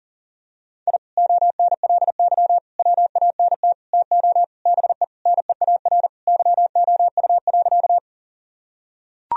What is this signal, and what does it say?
Morse code, heard as a keyed tone, about 40 wpm, 700 Hz. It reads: I ONLY WANT TO BE NEAR YOU.
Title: Sentences from Top 1000 Words - Review Part 1 - 40wpm